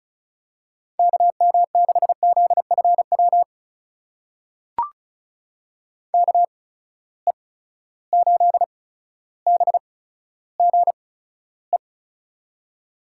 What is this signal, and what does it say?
Morse code, heard as a keyed tone, about 36 wpm, spaced as for 15 wpm, 700 Hz. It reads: KM6ZFW KE8BGE
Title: USA Call Signs - 6 Characters - 8x Character Spacing - 35wpm